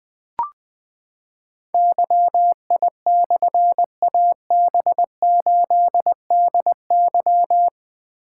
Morse code, heard as a keyed tone, about 20 wpm, 700 Hz. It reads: YI/AB8DY
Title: Difficult Call Signs 20wpm